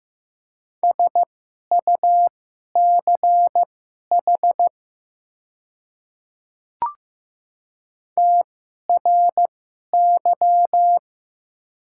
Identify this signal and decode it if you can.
Morse code, heard as a keyed tone, about 15 wpm, spaced as for 11 wpm, 700 Hz. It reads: SUCH TRY
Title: 2x Character Spacing - Top 200 Words - 15wpm